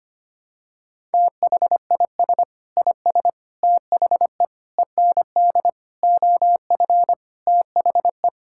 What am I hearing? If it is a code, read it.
Morse code, heard as a keyed tone, about 25 wpm, 700 Hz. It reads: THIS IS THE END OF THE
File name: practice / 25wpm / Sentences with 9-Word Phrases - 25wpm